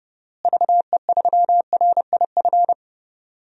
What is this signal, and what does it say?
Morse code, heard as a keyed tone, about 30 wpm, 700 Hz. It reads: VE3RIF
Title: DXCC-100 Call Signs - 6 Characters - Repeated 3x - 30wpm